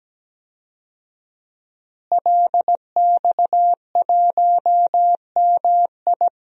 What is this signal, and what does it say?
Morse code, heard as a keyed tone, about 17 wpm, 700 Hz. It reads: LX1MI
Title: DXCC-100 Call Signs - 5 Characters - Repeated 2x - 17wpm